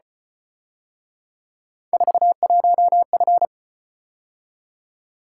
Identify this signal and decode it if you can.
Morse code, heard as a keyed tone, about 32 wpm, 700 Hz. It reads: 41F